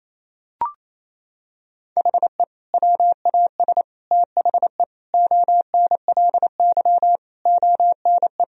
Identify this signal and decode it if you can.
Morse code, heard as a keyed tone, about 27 wpm, 700 Hz. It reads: HE WAS THE ONLY ONE
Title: Sentences from Top 1000 Words - Review Part 3 - 28wpm